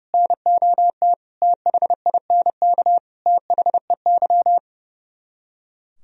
Morse code, heard as a keyed tone, about 37 wpm, 700 Hz. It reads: NOT THINK THEY
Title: Speed-Racing - Sets of 5 Words 30wpm